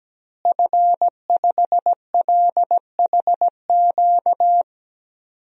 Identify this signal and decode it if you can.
Morse code, heard as a keyed tone, about 17 wpm, 700 Hz. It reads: F5LHQ
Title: DXCC-100 Call Signs - 5 Characters - Repeated 3x - 17wpm